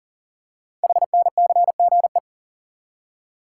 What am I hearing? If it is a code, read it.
Morse code, heard as a keyed tone, about 39 wpm, 700 Hz. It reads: HNCGE
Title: Code Groups - 5 Letters - 40wpm